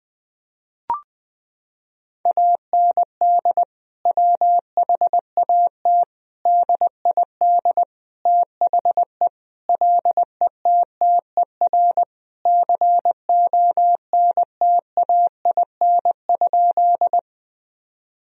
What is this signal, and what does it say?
Morse code, heard as a keyed tone, about 20 wpm, 700 Hz. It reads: AND WHAT DID THE LETTER CONTAIN?
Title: Sentences from Top 1000 Words - Review Part 2 - 20wpm